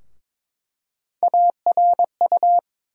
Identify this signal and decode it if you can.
Morse code, heard as a keyed tone, about 22 wpm, 700 Hz. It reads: ARU
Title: USA Call Sign Postfixes - 3 Letters 22wpm